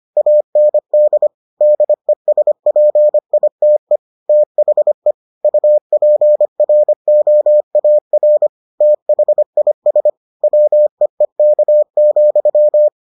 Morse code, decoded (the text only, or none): AND DESPITE THE UPROAR THIS WEEK,